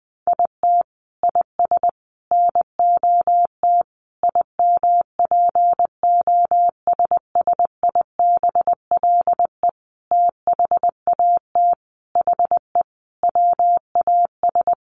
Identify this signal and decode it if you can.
Morse code, heard as a keyed tone, about 20 wpm, 700 Hz. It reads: IT IS NOT IMPOSSIBLE THAT HE WAS
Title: Sentences from Top 1100 Words 20wpm